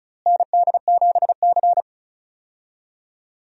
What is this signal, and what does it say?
Morse code, heard as a keyed tone, about 36 wpm, 700 Hz. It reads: ND7C
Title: USA Call Signs - 4 Characters - Repeated 3x - 35wpm